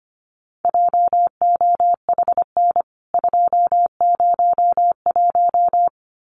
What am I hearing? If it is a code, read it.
Morse code, heard as a keyed tone, about 25 wpm, 700 Hz. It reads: JOHN 201
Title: CWT Exchange 25wpm